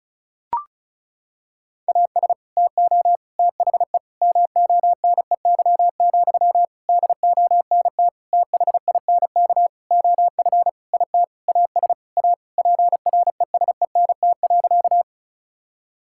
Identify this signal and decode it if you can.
Morse code, heard as a keyed tone, about 36 wpm, 700 Hz. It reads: AS TO THE MONEY, DONT THINK OF IT AS A PRESENT.
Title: Sentences from Top 1000 Words - Review Part 5 - 35wpm